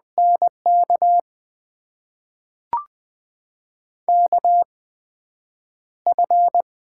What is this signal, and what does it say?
Morse code, heard as a keyed tone, about 21 wpm, spaced as for 9 wpm, 700 Hz. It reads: NK KF
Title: USA Call Signs - 6 Characters - 8x Character Spacing - 20wpm